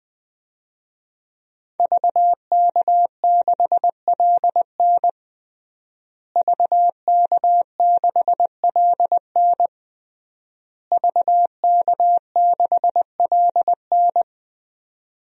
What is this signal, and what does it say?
Morse code, heard as a keyed tone, about 20 wpm, 700 Hz. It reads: VK6LN VK6LN VK6LN
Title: DXCC-100 Call Signs - 5 Characters - Repeated 3x - 20wpm